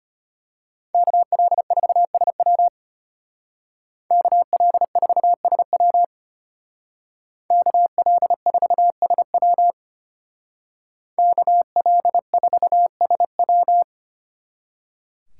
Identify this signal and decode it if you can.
Morse code, heard as a keyed tone, about 33 wpm, 700 Hz. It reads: KL4SW KL4SW KL4SW KL4SW